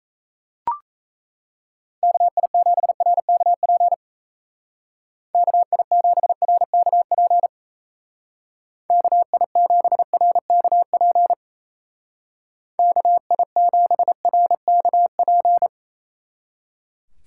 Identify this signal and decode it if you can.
Morse code, heard as a keyed tone, about 35 wpm, 700 Hz. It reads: KI7RKP KI7RKP KI7RKP KI7RKP